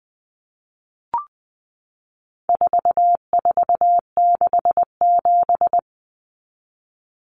Morse code, heard as a keyed tone, about 20 wpm, 700 Hz. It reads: 4467
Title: Numbers - 4-Digit 20wpm